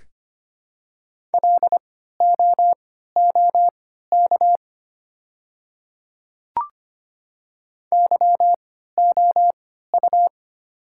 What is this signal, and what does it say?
Morse code, heard as a keyed tone, about 25 wpm, spaced as for 14 wpm, 700 Hz. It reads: LOOK YOU
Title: 3x Character Spacing - Top 200 Words - 25wpm